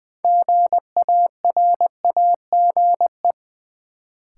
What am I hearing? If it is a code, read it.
Morse code, heard as a keyed tone, about 20 wpm, 700 Hz. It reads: GARAGE
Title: Sentences from Top 1600 Words - New Word Drill - 20wpm